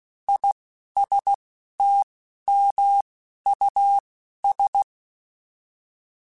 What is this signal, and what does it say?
Morse code, heard as a keyed tone, about 16 wpm, spaced as for 11 wpm, 800 Hz. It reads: ISTMUS
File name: Alphabetical / Alphabetical mp3s / Is @10